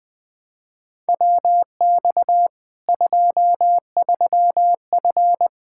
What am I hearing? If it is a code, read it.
Morse code, heard as a keyed tone, about 21 wpm, 700 Hz. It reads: WX 23F
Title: K1USN SST QSO Practice 20wpm